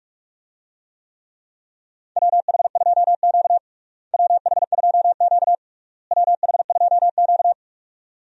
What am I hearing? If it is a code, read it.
Morse code, heard as a keyed tone, about 46 wpm, 700 Hz. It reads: WH2Q WH2Q WH2Q
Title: USA Call Signs - 4 Characters - Repeated 3x - 45wpm